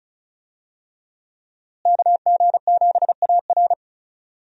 Morse code, heard as a keyed tone, about 36 wpm, 700 Hz. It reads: KG7AR
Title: USA Call Signs - 5 Characters - Repeated 3x - 35wpm